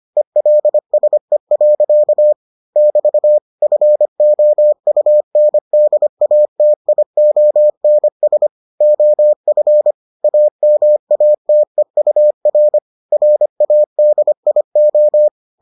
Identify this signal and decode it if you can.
Morse code, heard as a keyed tone, about 25 wpm, 600 Hz. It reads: ELSE. <BT> FOUNDATIONS OF AMATEUR RADIO